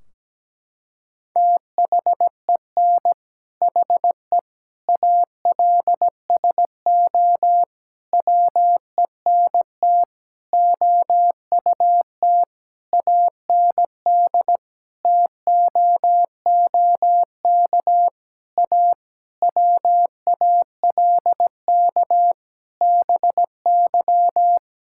Morse code, heard as a keyed tone, about 17 wpm, 700 Hz. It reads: THEN HE ALSO WENT OUT AND TOOK A WALK BY
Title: Sentences from Top 1600 Words 17wpm